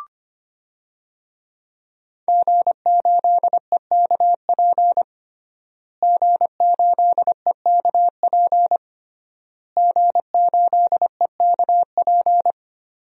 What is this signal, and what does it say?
Morse code, heard as a keyed tone, about 25 wpm, 700 Hz. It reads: G8EKP G8EKP G8EKP
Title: DXCC-100 Call Signs - 5 Characters - Repeated 3x - 25wpm